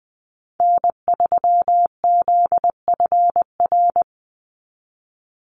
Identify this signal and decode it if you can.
Morse code, heard as a keyed tone, about 20 wpm, 700 Hz. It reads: N3ZFR
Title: USA Call Signs 20wpm